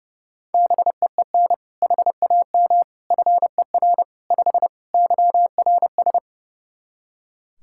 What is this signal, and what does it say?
Morse code, heard as a keyed tone, about 30 wpm, 700 Hz. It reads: BEEN HAM FER 5 YRS
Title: K1USN SST QSO Practice 30wpm